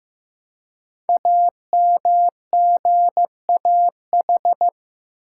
Morse code, heard as a keyed tone, about 15 wpm, 700 Hz. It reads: AMGAH
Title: Code Groups - 5 Letters - 15wpm